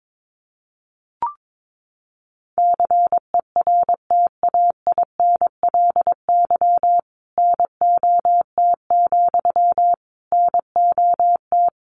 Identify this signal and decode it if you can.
Morse code, heard as a keyed tone, about 22 wpm, 700 Hz. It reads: CERTAINLY NOT, NOT
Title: Sentences from Top 900 Words 22wpm